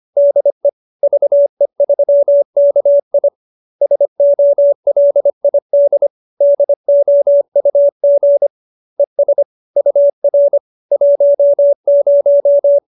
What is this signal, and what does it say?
Morse code, heard as a keyed tone, about 25 wpm, 570 Hz. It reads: DE VE3KI SOLID DOUG ES UR 10